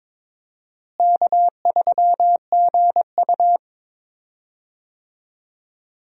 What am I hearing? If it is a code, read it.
Morse code, heard as a keyed tone, about 22 wpm, 700 Hz. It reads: K3GU
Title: USA Call Signs - 4 Characters - Repeated 3x - 22wpm